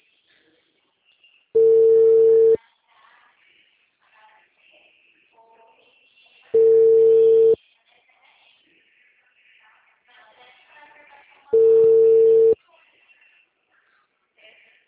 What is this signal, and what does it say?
Ringback tone (China, 450 Hz).